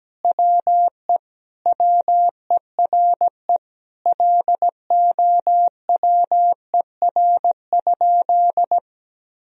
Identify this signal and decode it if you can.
Morse code, heard as a keyed tone, about 17 wpm, 700 Hz. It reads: WE WERE LOWER?
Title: Sentences from Top 1000 Words - Review Part 4 - 17wpm